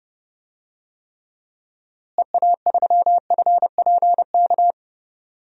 Morse code, heard as a keyed tone, about 30 wpm, 700 Hz. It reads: EA3FPK
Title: DXCC-100 Call Signs - 6 Characters - Repeated 2x - 30wpm